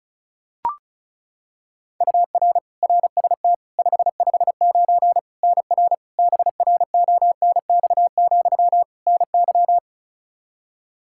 Morse code, heard as a keyed tone, about 36 wpm, 700 Hz. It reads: UR RST 559 NR BRONX, NY
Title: K1USN SST QSO Practice 35wpm